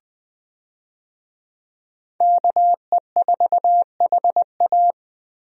Morse code, heard as a keyed tone, about 20 wpm, 700 Hz. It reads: KE4HA
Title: USA Call Signs - 5 Characters - Repeated 3x - 20wpm